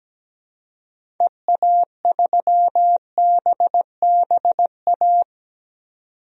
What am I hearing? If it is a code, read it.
Morse code, heard as a keyed tone, about 17 wpm, 700 Hz. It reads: EA3BBA
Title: DXCC-100 Call Signs - 6 Characters - 17wpm